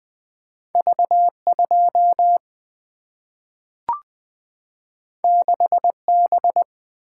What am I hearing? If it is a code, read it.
Morse code, heard as a keyed tone, about 20 wpm, 700 Hz. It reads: V2 6B
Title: Single-Letter Single-Number 20wpm